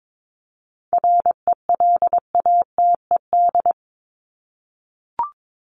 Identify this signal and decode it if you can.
Morse code, heard as a keyed tone, about 22 wpm, 700 Hz. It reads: RELATED
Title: ED Suffix - Top 100 Words with Base Words 22wpm